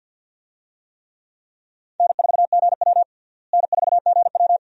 Morse code, heard as a keyed tone, about 53 wpm, 700 Hz. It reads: N4GW N4GW
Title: USA Call Signs - 4 Characters - Repeated 2x - 50wpm